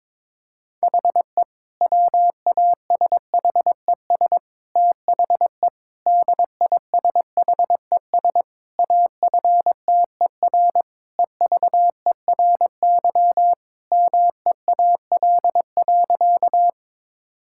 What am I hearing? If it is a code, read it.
Morse code, heard as a keyed tone, about 22 wpm, 700 Hz. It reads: HE WASHES THE DISHES AFTER EVERY MEAL.
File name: Difficult Sentences with Hints - 22wpm